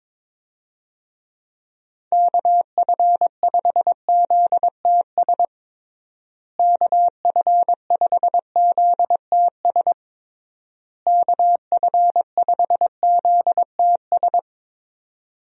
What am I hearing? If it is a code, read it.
Morse code, heard as a keyed tone, about 22 wpm, 700 Hz. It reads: KF5ZTS KF5ZTS KF5ZTS
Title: USA Call Signs - 6 Characters - Repeated 3x - 22wpm